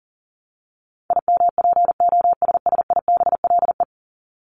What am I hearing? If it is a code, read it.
Morse code, heard as a keyed tone, about 40 wpm, 700 Hz. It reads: IMPOSSIBLE
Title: Top 1001-1250 Words 40wpm